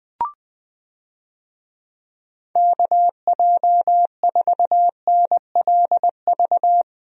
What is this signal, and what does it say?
Morse code, heard as a keyed tone, about 21 wpm, 700 Hz. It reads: KJ4NLV
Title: USA Call Signs - 6 Characters - Repeated 2x - 20wpm